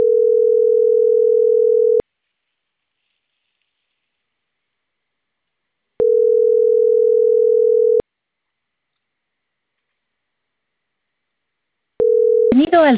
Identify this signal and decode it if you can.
Ringback tone (North America), 440 + 480 Hz.